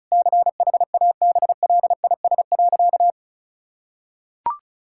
Morse code, heard as a keyed tone, about 36 wpm, 700 Hz. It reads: CHABLIS.
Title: Pangram Challenge v2 - 35wpm